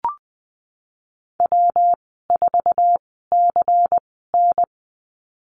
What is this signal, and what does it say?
Morse code, heard as a keyed tone, about 19 wpm, spaced as for 14 wpm, 700 Hz. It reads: W4CN
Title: 207 Lesson 33g - Letter j - Call Signs 20wpm